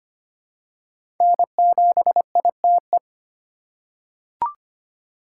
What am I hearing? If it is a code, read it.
Morse code, heard as a keyed tone, about 25 wpm, 700 Hz. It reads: N7ITE
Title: USA Call Signs - 3x Character Spacing - 25wpm